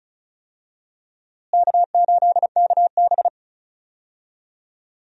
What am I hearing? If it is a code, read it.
Morse code, heard as a keyed tone, about 36 wpm, 700 Hz. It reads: K8KB
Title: USA Call Signs - 4 Characters - Repeated 3x - 35wpm